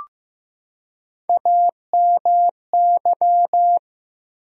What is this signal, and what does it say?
Morse code, heard as a keyed tone, about 15 wpm, 700 Hz. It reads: AMY
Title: USA Common Names - Max 4 Letters - 15wpm